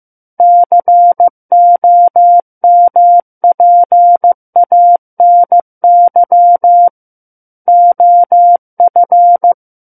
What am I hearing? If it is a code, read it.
Morse code, heard as a keyed tone, about 15 wpm, 700 Hz. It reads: COMPANY OF
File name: Book One / 06-Chapter 6 - 15 WPM